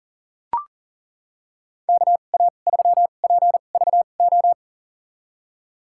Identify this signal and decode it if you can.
Morse code, heard as a keyed tone, about 38 wpm, spaced as for 28 wpm, 700 Hz. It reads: KA3PVO
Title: USA Call Signs - 6 Characters - 2x Character Spacing - 40wpm